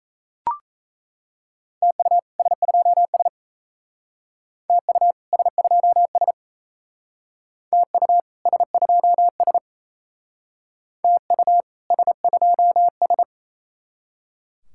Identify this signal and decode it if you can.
Morse code, heard as a keyed tone, about 37 wpm, 700 Hz. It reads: TU S2S TU S2S TU S2S TU S2S